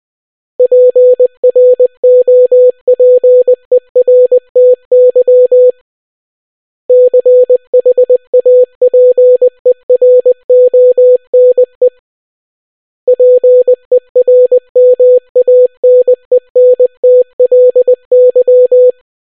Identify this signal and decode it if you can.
Morse code, heard as a keyed tone, about 20 wpm, 500 Hz. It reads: PROPERTY CHAPERONE PERMANENTLY